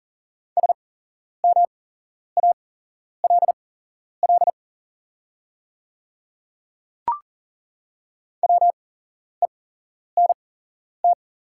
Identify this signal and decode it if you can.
Morse code, heard as a keyed tone, about 40 wpm, spaced as for 11 wpm, 700 Hz. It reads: SMALL WENT
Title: 8x Character Spacing - Top 200 Words - 40wpm